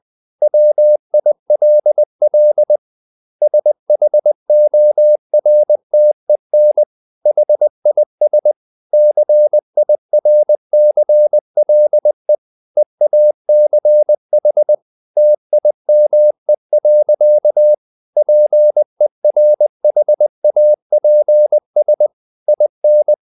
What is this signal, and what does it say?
Morse code, heard as a keyed tone, about 20 wpm, 600 Hz. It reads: WILL SHORTEN HIS CIRCLE EACH TIME. PERHAPS IN
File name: the old man and the sea part 2 20wpm